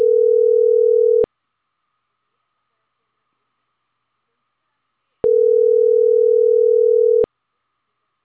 Ringback tone (North America), 440 + 480 Hz.